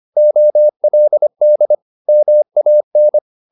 Morse code, heard as a keyed tone, about 25 wpm, 600 Hz. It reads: OLD MAN